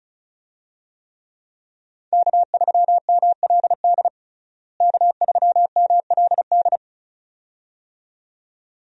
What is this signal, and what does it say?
Morse code, heard as a keyed tone, about 36 wpm, 700 Hz. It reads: K3MLD K3MLD
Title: USA Call Signs - 5 Characters - Repeated 2x - 35wpm